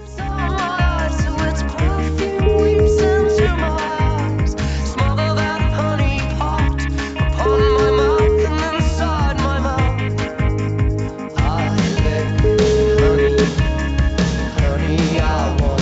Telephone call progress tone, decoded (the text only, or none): ringback tone